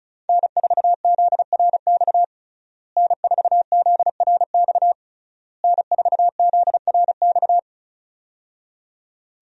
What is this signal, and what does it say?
Morse code, heard as a keyed tone, about 36 wpm, 700 Hz. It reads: N4ZRX N4ZRX N4ZRX
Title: USA Call Signs - 5 Characters - Repeated 3x - 35wpm